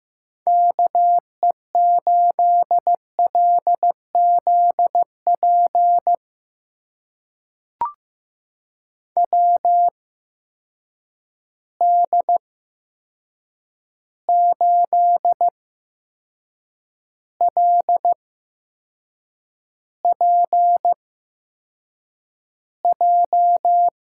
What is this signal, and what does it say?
Morse code, heard as a keyed tone, about 15 wpm, spaced as for 6 wpm, 700 Hz. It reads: KE8LZP WD8LPJ
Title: USA Call Signs - 6 Characters - 8x Character Spacing - 15wpm